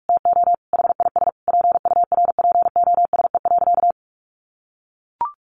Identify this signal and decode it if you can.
Morse code, heard as a keyed tone, about 44 wpm, 700 Hz. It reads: TO HIS PURPOSE.
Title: Sentences from Top 1400 Words 45wpm